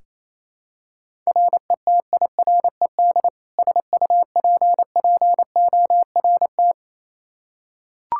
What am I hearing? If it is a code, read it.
Morse code, heard as a keyed tone, about 28 wpm, 700 Hz. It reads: RETIRED SUPPORT
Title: K1USN SST QSO Practice 28wpm